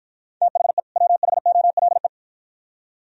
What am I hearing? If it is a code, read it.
Morse code, heard as a keyed tone, about 53 wpm, 700 Hz. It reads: THE WHOLE